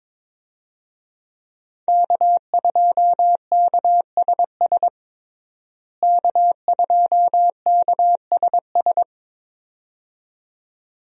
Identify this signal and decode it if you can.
Morse code, heard as a keyed tone, about 22 wpm, 700 Hz. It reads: K2KSS K2KSS
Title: USA Call Signs - 5 Characters - Repeated 2x - 22wpm